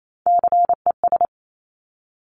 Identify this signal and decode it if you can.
Morse code, heard as a keyed tone, about 28 wpm, 700 Hz. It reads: CES
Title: Top Three Letter Combinations in Words 28wpm